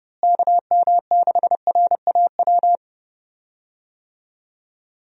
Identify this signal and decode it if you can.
Morse code, heard as a keyed tone, about 31 wpm, 700 Hz. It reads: KM6RAW
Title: USA Call Signs - 6 Characters - Repeated 3x - 30wpm